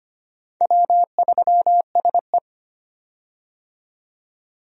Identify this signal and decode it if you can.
Morse code, heard as a keyed tone, about 25 wpm, 700 Hz. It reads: W3SE
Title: USA Call Signs - 4 Characters - 25wpm